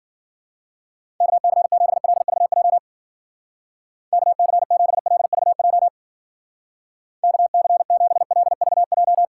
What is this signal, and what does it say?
Morse code, heard as a keyed tone, about 49 wpm, 700 Hz. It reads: KC7LVJ KC7LVJ KC7LVJ
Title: Speed Racing - USA 2x3 Call Signs 40wpm